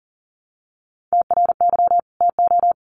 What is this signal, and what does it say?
Morse code, heard as a keyed tone, about 40 wpm, 700 Hz. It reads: TRY TO
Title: Sets of 2 Words 40wpm